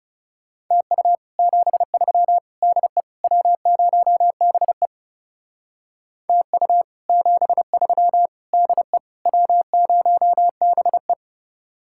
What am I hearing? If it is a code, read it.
Morse code, heard as a keyed tone, about 38 wpm, 700 Hz. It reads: TU 73 DE W0BE TU 73 DE W0BE